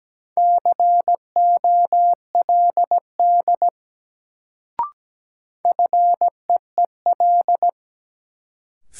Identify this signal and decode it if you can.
Morse code, heard as a keyed tone, about 17 wpm, 700 Hz. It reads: COLD FEEL
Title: Sets of 5 Words - New Word Drill - 17wpm